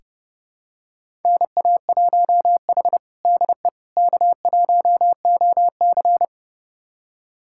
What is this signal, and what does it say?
Morse code, heard as a keyed tone, about 30 wpm, 700 Hz. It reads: NA1H DE K1OC
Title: USA Callsign DE Callsign - 4 Characters - 30wpm